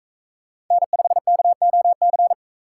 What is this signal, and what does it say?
Morse code, heard as a keyed tone, about 44 wpm, 700 Hz. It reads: N5KOC